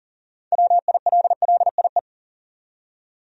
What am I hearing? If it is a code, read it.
Morse code, heard as a keyed tone, about 40 wpm, 700 Hz. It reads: WILLIE
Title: USA Common Names - 40wpm